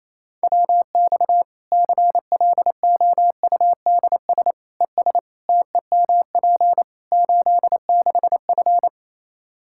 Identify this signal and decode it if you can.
Morse code, heard as a keyed tone, about 28 wpm, 700 Hz. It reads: WX CLOUDS ES TEMP 86F
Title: Practice QSO - Exchange 2 - Info - 28wpm